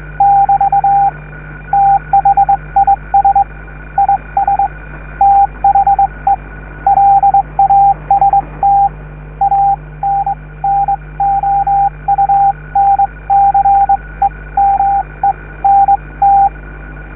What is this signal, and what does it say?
Morse code, heard as a keyed tone, about 22 wpm, spaced as for 19 wpm, 800 Hz, with radio noise. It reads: <BT> THIS IS THE LAST ANNOUNCEMENT